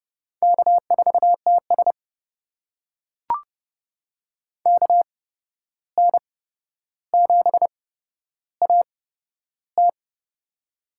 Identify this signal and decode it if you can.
Morse code, heard as a keyed tone, about 30 wpm, spaced as for 12 wpm, 700 Hz. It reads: K4TS KN7AT
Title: USA Call Signs - 8x Character Spacing - 30wpm